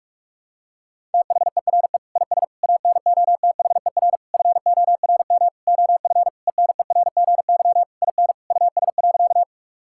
Morse code, heard as a keyed tone, about 41 wpm, 700 Hz. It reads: THERE IS ANOTHER FORM OF ENERGY IN US.
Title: Sentences from Top 1000 Words - Review Part 6 - 45wpm